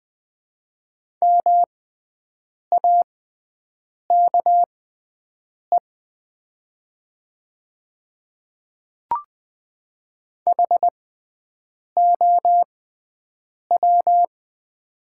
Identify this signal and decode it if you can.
Morse code, heard as a keyed tone, about 20 wpm, spaced as for 7 wpm, 700 Hz. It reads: MAKE HOW
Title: 6x Character Spacing - Top 100 Words - 20wpm